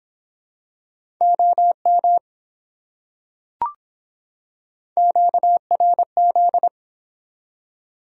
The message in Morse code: OM QRZ

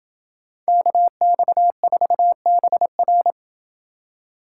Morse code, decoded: KX4BR

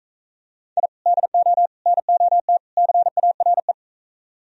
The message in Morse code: I DO NOT CARE

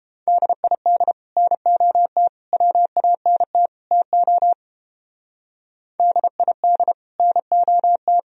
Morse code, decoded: DID NOT WANT TO DID NOT

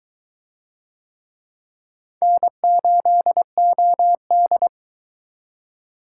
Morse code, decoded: N8OD

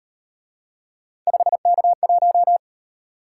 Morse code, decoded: 5K1

